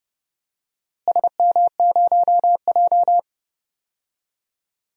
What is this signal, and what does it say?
Morse code, heard as a keyed tone, about 31 wpm, 700 Hz. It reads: SM0J
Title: DXCC-100 Call Signs - 4 Characters - Repeated 3x - 30wpm